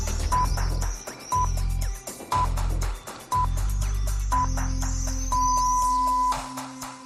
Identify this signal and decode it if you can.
Radio time signal (time pips). It is heard from 0.3 to 6.3 s.